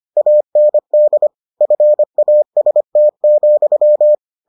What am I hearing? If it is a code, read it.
Morse code, heard as a keyed tone, about 25 wpm, 600 Hz. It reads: AND FAST,